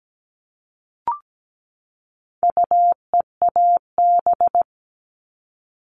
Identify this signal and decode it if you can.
Morse code, heard as a keyed tone, about 17 wpm, 700 Hz. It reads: UEAB
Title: All Cut Numbers - 4-Digit 17wpm